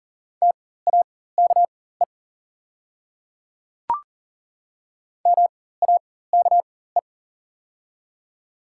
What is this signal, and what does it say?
Morse code, heard as a keyed tone, about 41 wpm, spaced as for 19 wpm, 700 Hz. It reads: TAKE MAKE
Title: 4x Character Spacing - Top 300 Words - 40wpm